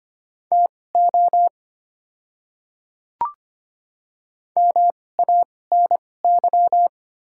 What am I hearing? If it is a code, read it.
Morse code, heard as a keyed tone, about 25 wpm, spaced as for 18 wpm, 700 Hz. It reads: TO MANY
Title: 2x Character Spacing - Top 100 Words - 25wpm